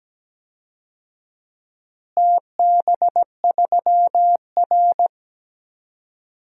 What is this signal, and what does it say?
Morse code, heard as a keyed tone, about 17 wpm, 700 Hz. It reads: TB3R
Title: DXCC-100 Call Signs - 4 Characters - Repeated 3x - 17wpm